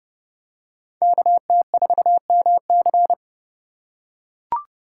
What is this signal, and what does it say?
Morse code, heard as a keyed tone, about 31 wpm, 700 Hz. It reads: KT4MC
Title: USA Call Signs - 5 Characters - 3x Character Spacing - 30wpm